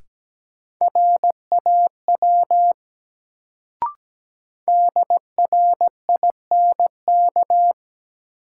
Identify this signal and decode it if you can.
Morse code, heard as a keyed tone, about 17 wpm, 700 Hz. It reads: RAW DRINK
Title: Sentences from Top 1500 Words - New Word Drill - 17wpm